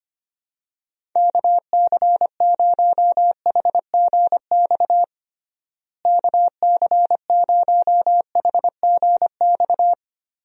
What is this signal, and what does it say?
Morse code, heard as a keyed tone, about 25 wpm, 700 Hz. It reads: KC0HGX KC0HGX
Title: USA Call Signs - 6 Characters - Repeated 3x - 25wpm